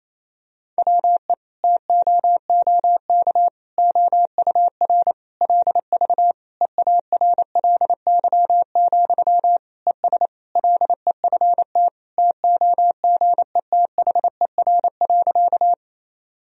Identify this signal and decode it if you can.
Morse code, heard as a keyed tone, about 28 wpm, 700 Hz. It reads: WE TOOK OUR LV EARLY, ES LEFT TOGETHER.
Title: Sentences from Top 400 Words and Abbreviations 28wpm